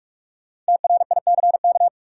Morse code, heard as a keyed tone, about 45 wpm, 700 Hz. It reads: TRICK
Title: Sentences from Top 1500 Words - New Word Drill Alt - 45wpm